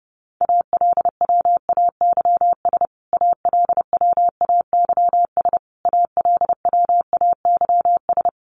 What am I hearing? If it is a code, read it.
Morse code, heard as a keyed tone, about 30 wpm, 700 Hz. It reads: ALWAYS ALWAYS ALWAYS
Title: Top 500 Words Repeated 30wpm